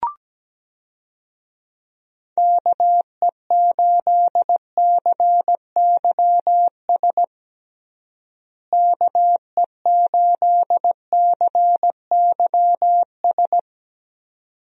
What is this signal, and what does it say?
Morse code, heard as a keyed tone, about 17 wpm, 700 Hz. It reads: KE8CYS KE8CYS
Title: USA Call Signs - 6 Characters - Repeated 2x - 17wpm